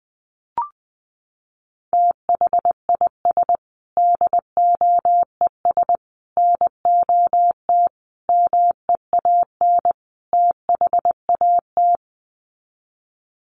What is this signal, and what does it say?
Morse code, heard as a keyed tone, about 19 wpm, 700 Hz. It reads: THIS DOES NOT MEAN THAT
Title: Sets of 5 Words Encore2 20wpm